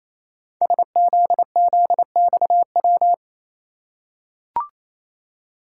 Morse code, heard as a keyed tone, about 27 wpm, 700 Hz. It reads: SZZXW
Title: Code Groups - 5 Letters - 28wpm